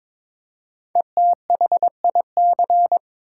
Morse code, heard as a keyed tone, about 22 wpm, 700 Hz. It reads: ETHIC
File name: 5 Letter Words - First 3 Letters Given - 22wpm